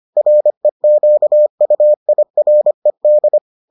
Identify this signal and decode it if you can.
Morse code, heard as a keyed tone, about 25 wpm, 600 Hz. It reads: REQUIRED